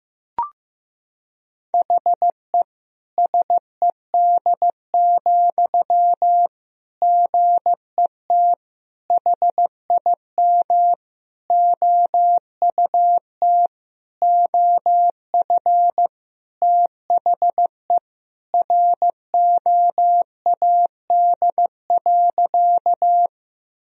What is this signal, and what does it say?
Morse code, heard as a keyed tone, about 15 wpm, 700 Hz. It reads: HE SED, GET HIM OUT OF THE ROAD.
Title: Sentences from Top 400 Words and Abbreviations 15wpm